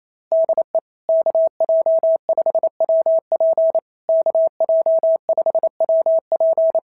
Morse code, heard as a keyed tone, about 27 wpm, 660 Hz. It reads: DE KJ5WP KJ5WP
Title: SOTA QSO Practice 28wpm